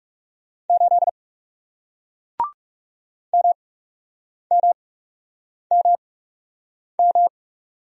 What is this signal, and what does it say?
Morse code, heard as a keyed tone, about 38 wpm, spaced as for 8 wpm, 700 Hz. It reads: 8 MMMM